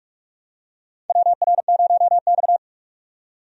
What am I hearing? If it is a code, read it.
Morse code, heard as a keyed tone, about 46 wpm, 700 Hz. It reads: WR0X
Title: USA Call Signs - 4 Characters - 45wpm